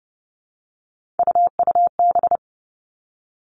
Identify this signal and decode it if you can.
Morse code, heard as a keyed tone, about 30 wpm, 700 Hz. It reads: UUB
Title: Three Challenging Letter-Number Combinations 30wpm